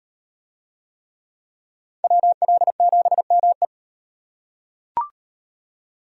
Morse code, heard as a keyed tone, about 34 wpm, 700 Hz. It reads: WL7ME